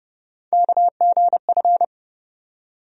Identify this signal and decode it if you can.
Morse code, heard as a keyed tone, about 30 wpm, 700 Hz. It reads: KGF
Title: USA Call Sign Postfixes - 3 Letters 30wpm